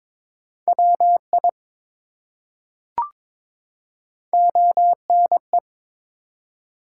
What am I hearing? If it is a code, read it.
Morse code, heard as a keyed tone, about 22 wpm, 700 Hz. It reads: WI ONE
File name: ARRL Sections v2 - 22wpm